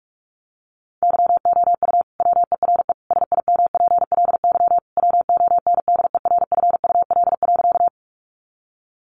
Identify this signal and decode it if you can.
Morse code, heard as a keyed tone, about 44 wpm, 700 Hz. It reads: YOU WERE SIMPLY WONDERFUL.
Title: Sentences from Top 1100 Words 45wpm